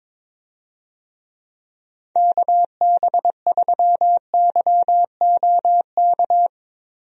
Morse code, heard as a keyed tone, about 22 wpm, 700 Hz. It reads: KB3YOK
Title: USA Call Signs - 6 Characters - Repeated 3x - 22wpm